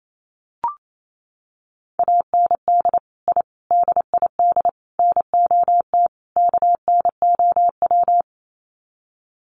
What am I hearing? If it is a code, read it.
Morse code, heard as a keyed tone, about 28 wpm, 700 Hz. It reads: AND I DID NOT KNOW
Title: Sets of 5 Words Encore 28wpm